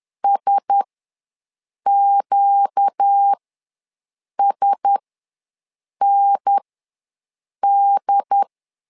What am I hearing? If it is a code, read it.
Morse code, heard as a keyed tone, about 10 wpm, spaced as for 6 wpm, 800 Hz. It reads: SQSND
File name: LEZIONI / 4 Q1S ws 10wpm 2208